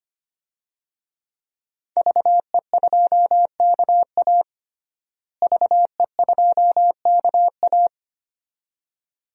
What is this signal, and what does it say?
Morse code, heard as a keyed tone, about 25 wpm, 700 Hz. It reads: VE2KA VE2KA
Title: DXCC-100 Call Signs - 5 Characters - Repeated 2x - 25wpm